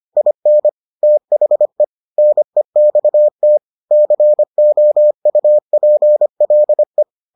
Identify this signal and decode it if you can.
Morse code, heard as a keyed tone, about 25 wpm, 600 Hz. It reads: IN THE NEXT COUPLE